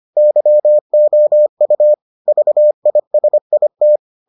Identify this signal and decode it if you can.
Morse code, heard as a keyed tone, about 25 wpm, 600 Hz. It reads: YOU VISIT